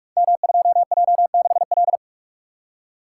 Morse code, heard as a keyed tone, about 44 wpm, 700 Hz. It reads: M2J6L
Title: Code Groups - 5 Letter:Numbers - 45wpm